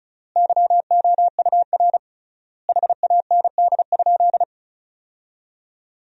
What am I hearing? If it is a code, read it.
Morse code, heard as a keyed tone, about 34 wpm, 700 Hz. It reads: YOUR HAND?
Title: 3x Word-Spacing - Sentences from Top 200 Words - 35wpm